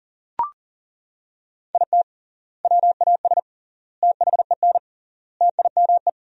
Morse code, heard as a keyed tone, about 40 wpm, 700 Hz. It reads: IT WAS THEN TIME
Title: 3x Word-Spacing - Sentences from Top 300 Words - 40wpm